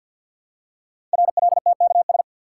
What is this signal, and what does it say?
Morse code, heard as a keyed tone, about 52 wpm, 700 Hz. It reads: RLTKS
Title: Code Groups - 5 Letters - 50wpm